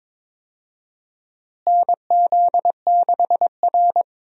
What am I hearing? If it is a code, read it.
Morse code, heard as a keyed tone, about 21 wpm, 700 Hz. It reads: NZ6R
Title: USA Call Signs - 4 Characters - Repeated 2x - 22wpm